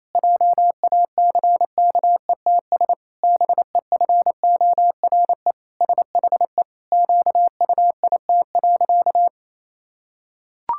Morse code, heard as a keyed tone, about 28 wpm, 700 Hz. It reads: JACKETS BEFORE SHE QUIT.
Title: Pangram Challenge v2 - 28wpm